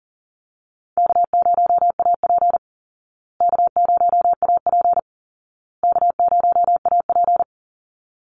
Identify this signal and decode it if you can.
Morse code, heard as a keyed tone, about 40 wpm, 700 Hz. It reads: K0AP K0AP K0AP
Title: Call Signs Repeated 2x - 40wpm